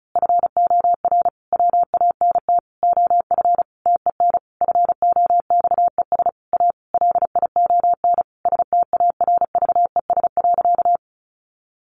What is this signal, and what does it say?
Morse code, heard as a keyed tone, about 34 wpm, 700 Hz. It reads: FOR WANT OF TEN FOXES A LION STARVES.
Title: A Sound Of Thunder 35wpm